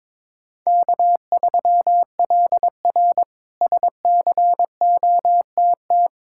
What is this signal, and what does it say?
Morse code, heard as a keyed tone, about 21 wpm, 700 Hz. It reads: K3LR SCOTT
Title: K1USN SST QSO Practice - 22wpm